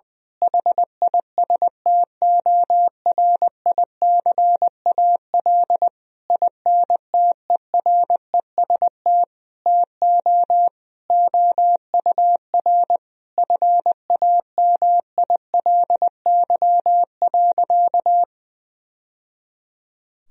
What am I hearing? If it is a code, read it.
Morse code, heard as a keyed tone, about 20 wpm, 700 Hz. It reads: HISTORICAL INTEREST TO OUR FAMILY.
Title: Sentences from Top 1000 Words - Review Part 4 - 20wpm